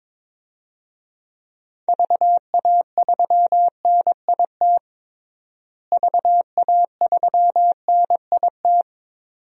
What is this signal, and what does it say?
Morse code, heard as a keyed tone, about 22 wpm, 700 Hz. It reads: VA3NIT VA3NIT
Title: DXCC-100 Call Signs - 6 Characters - Repeated 2x - 22wpm